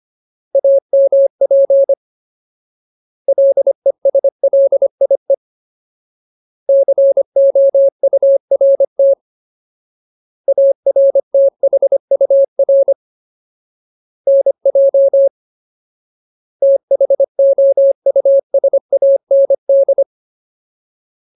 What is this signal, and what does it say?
Morse code, heard as a keyed tone, about 25 wpm, 550 Hz. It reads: AMP LESLIE COURT ARTHUR NJ THOUSAND
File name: Ultimate Word List / 25WPM Ultimate